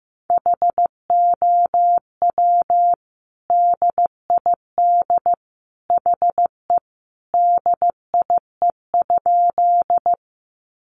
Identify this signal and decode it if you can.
Morse code, heard as a keyed tone, about 15 wpm, 700 Hz. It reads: HOW DID HE DIE?
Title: Sentences from Top 1300 Words 15wpm